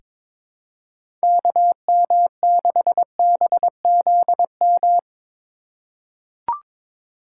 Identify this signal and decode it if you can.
Morse code, heard as a keyed tone, about 22 wpm, 700 Hz. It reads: KM6BZM
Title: USA Call Signs - 6 Characters - 4x Character Spacing - 22wpm